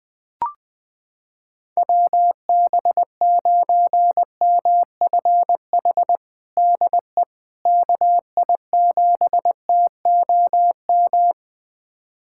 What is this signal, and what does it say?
Morse code, heard as a keyed tone, about 20 wpm, 700 Hz. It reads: WB9MFH DE KI7TOM
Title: USA Callsign DE Callsign - 6 Characters - 20wpm